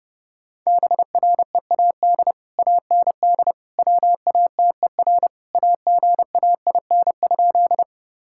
Morse code, heard as a keyed tone, about 32 wpm, 700 Hz. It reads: BREAD AND WATER AGAIN?
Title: Sentences from Top 1600 Words 30wpm